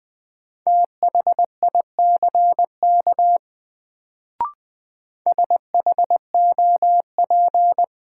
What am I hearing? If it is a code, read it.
Morse code, heard as a keyed tone, about 20 wpm, 700 Hz. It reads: THICK SHOP
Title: Sentences from Top 1600 Words - New Word Drill - 20wpm